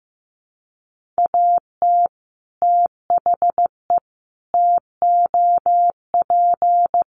Morse code, heard as a keyed tone, about 15 wpm, 700 Hz. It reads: AT THE TOP
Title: Sets of 4 Words 15wpm